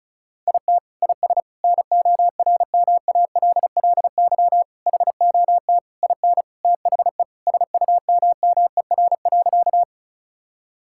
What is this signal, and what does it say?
Morse code, heard as a keyed tone, about 34 wpm, 700 Hz. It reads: IT IS NORMALLY HOT IN THE SUMMER.
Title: Sentences from Top 1000 Words - Review Part 6 - 35wpm